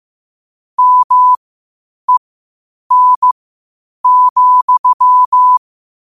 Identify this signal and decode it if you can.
Morse code, heard as a keyed tone, about 15 wpm, spaced as for 8 wpm, 1000 Hz. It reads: MEN,
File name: qotdfiles / 8wpm